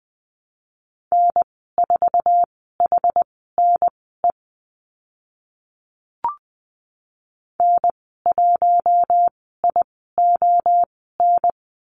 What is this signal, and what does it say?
Morse code, heard as a keyed tone, about 20 wpm, spaced as for 14 wpm, 700 Hz. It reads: N4HNE N1ION
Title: 062 Lesson 11g - Letter h - Call Signs 20wpm